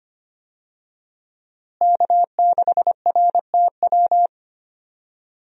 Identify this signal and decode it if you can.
Morse code, heard as a keyed tone, about 25 wpm, 700 Hz. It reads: K6RTW
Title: USA Call Signs - 5 Characters - Repeated 2x - 25wpm